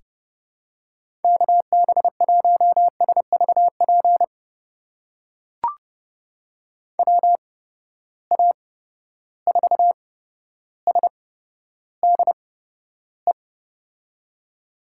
Morse code, heard as a keyed tone, about 31 wpm, spaced as for 13 wpm, 700 Hz. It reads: KB1SVP WA4SDE
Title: USA Call Signs - 6 Characters - 8x Character Spacing - 30wpm